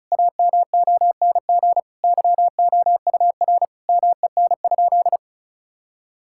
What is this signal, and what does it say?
Morse code, heard as a keyed tone, about 34 wpm, 700 Hz. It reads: AMONG YOUR MEN?
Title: Sentences from Top 1000 Words - Review Part 6 - 35wpm